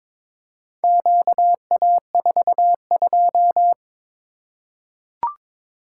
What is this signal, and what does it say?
Morse code, heard as a keyed tone, about 22 wpm, 700 Hz. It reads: QA42
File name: practice / 22wpm / Four Letter-Number Combinations - 22wpm